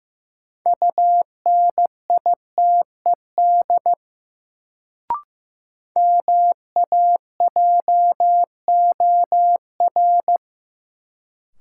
Morse code, heard as a keyed tone, about 15 wpm, 700 Hz. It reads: UNITED MAJOR
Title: Sentences from Top 600 Words - New Word Drill - 15wpm